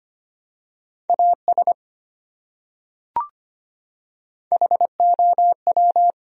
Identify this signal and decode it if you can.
Morse code, heard as a keyed tone, about 25 wpm, 700 Hz. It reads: AS HOW
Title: Top 100 Words 25wpm